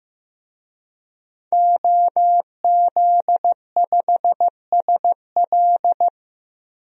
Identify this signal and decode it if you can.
Morse code, heard as a keyed tone, about 15 wpm, 700 Hz. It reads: OZ5SL
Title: DXCC-100 Call Signs - 5 Characters - Repeated 2x - 15wpm